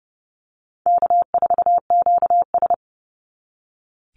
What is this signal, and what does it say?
Morse code, heard as a keyed tone, about 30 wpm, 700 Hz. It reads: K4QS
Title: NA SOTA Chaser Callsigns 30wpm